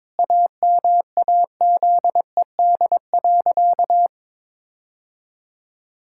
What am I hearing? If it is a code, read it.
Morse code, heard as a keyed tone, about 22 wpm, 700 Hz. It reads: AMAZED.
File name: practice / 22wpm / Sentences with 4-Word Phrases - 22wpm